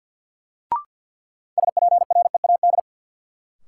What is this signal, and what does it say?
Morse code, heard as a keyed tone, about 53 wpm, 700 Hz. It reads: SPREAD
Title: Sentences from Top 1200 Words - New Word Drill Alt - 50wpm